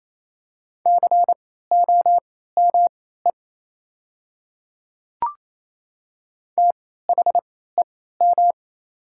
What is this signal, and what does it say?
Morse code, heard as a keyed tone, about 27 wpm, spaced as for 16 wpm, 690 Hz. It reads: COME THEM
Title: 3x Character Spacing - Top 100 Words - 28wpm